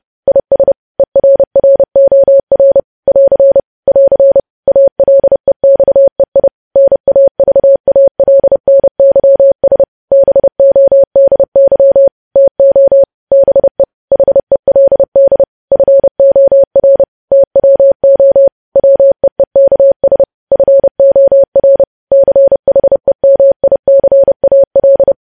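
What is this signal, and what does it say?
Morse code, heard as a keyed tone, about 30 wpm, 550 Hz. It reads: IS ERROR <AR> <AR> ALEXEI NAVALNYS BODY TO BE HELD FOR TWO WEEKS FOR CHEMICAL
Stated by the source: Morse news headlines sent at 30.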